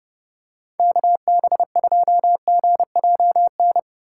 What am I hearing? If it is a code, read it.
Morse code, heard as a keyed tone, about 30 wpm, 700 Hz. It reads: KB2GJN
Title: USA Call Signs - 6 Characters - 4x Character Spacing - 30wpm